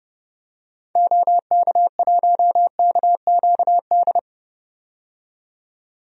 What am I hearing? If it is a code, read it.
Morse code, heard as a keyed tone, about 29 wpm, 700 Hz. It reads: OK1KQD
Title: DXCC-100 Call Signs - 6 Characters - 30wpm